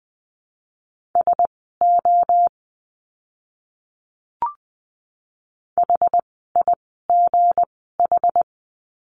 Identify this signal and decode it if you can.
Morse code, heard as a keyed tone, about 20 wpm, spaced as for 15 wpm, 700 Hz. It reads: SO HIGH
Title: 230 Lesson 37f - Letter x - Words 20wpm